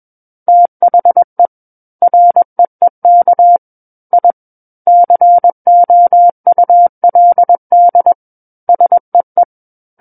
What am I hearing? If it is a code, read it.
Morse code, heard as a keyed tone, about 21 wpm, 700 Hz. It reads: THE REEK I COULD SEE
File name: Book One / 12-Chapter 12 - 21 WPM